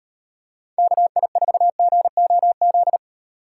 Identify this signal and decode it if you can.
Morse code, heard as a keyed tone, about 36 wpm, 700 Hz. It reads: KI4GOZ